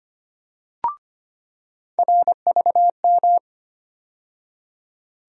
Morse code, heard as a keyed tone, about 25 wpm, 700 Hz. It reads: RVM
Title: USA Call Sign Postfixes - 3 Letters 25wpm